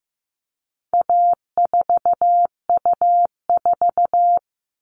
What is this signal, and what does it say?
Morse code, heard as a keyed tone, about 15 wpm, 700 Hz. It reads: A4U4
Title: All Cut Numbers - 4-Digit 15wpm